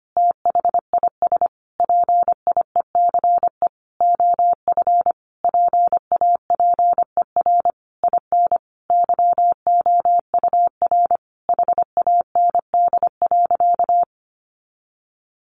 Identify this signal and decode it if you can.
Morse code, heard as a keyed tone, about 25 wpm, 700 Hz. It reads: THIS PIECE OF PAPER IN YOUR HAND.
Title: Sentences from Top 400 Words 25wpm